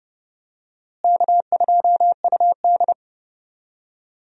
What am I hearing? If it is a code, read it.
Morse code, heard as a keyed tone, about 30 wpm, 700 Hz. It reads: K2UD
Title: USA Call Signs - 4 Characters - Repeated 2x - 30wpm